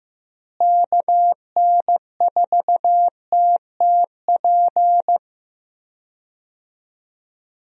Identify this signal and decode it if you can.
Morse code, heard as a keyed tone, about 16 wpm, 700 Hz. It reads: KN4TTP
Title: USA Call Signs - 6 Characters - Repeated 2x - 15wpm